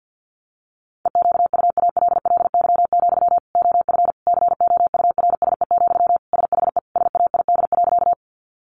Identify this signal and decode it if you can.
Morse code, heard as a keyed tone, about 50 wpm, 700 Hz. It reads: EQUALLY, OF COURSE, SHE SAID.
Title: Sentences from Top 1400 Words 50wpm